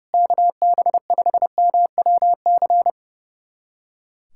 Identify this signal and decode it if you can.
Morse code, heard as a keyed tone, about 32 wpm, 700 Hz. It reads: KB5MWC
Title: USA Callsign DE Callsign - 6 Characters - 30wpm